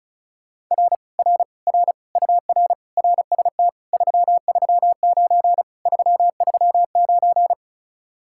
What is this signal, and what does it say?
Morse code, heard as a keyed tone, about 39 wpm, 700 Hz. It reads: R R R UR RST 339 339